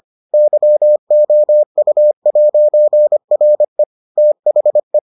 Morse code, heard as a keyed tone, about 25 wpm, 600 Hz. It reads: YOU'RE THE